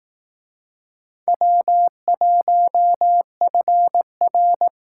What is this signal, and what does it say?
Morse code, heard as a keyed tone, about 18 wpm, 700 Hz. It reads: W1FR
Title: Speed Racing - USA 1x2 Call Signs 15wpm